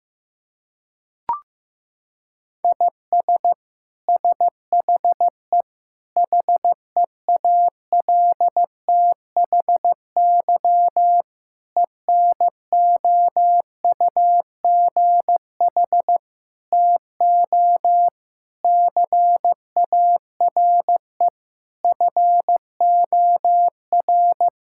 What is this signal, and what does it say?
Morse code, heard as a keyed tone, about 16 wpm, 700 Hz. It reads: IS SHE HEALTHY ENOUGH TO CARE FOR
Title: Sentences from Top 1000 Words - Review Part 5 - 15wpm